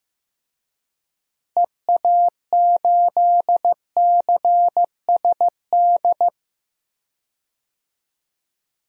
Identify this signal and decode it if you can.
Morse code, heard as a keyed tone, about 15 wpm, 700 Hz. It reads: EA8CSD
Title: DXCC-100 Call Signs - 6 Characters - Repeated 3x - 15wpm